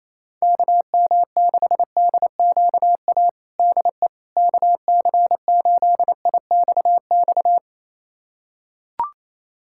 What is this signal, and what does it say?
Morse code, heard as a keyed tone, about 28 wpm, 700 Hz. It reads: KM6DQA DE KC8IXX
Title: USA Callsign DE Callsign 28wpm